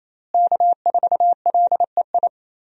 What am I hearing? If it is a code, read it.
Morse code, heard as a keyed tone, about 28 wpm, 700 Hz. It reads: K4LEI
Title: USA Call Signs - 5 Characters - Repeated 3x - 28wpm